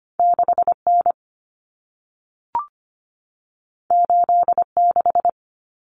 Morse code, heard as a keyed tone, about 25 wpm, 700 Hz. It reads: 6N 86
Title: Common Cut Numbers - 2-Digit 25wpm